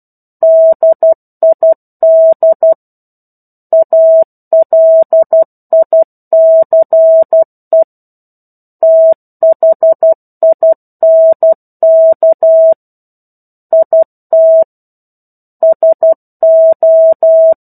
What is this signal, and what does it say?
Morse code, heard as a keyed tone, about 12 wpm, 650 Hz. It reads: DID ALICE THINK IT SO
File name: 01-Chapter-01 12 WPM